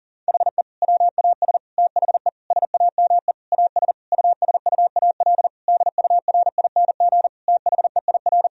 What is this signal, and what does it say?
Morse code, heard as a keyed tone, about 40 wpm, 700 Hz. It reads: HE WAS THE SAME AS USUAL DURING THEIR
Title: Sentences from Top 1000 Words - Review Part 2 - 40wpm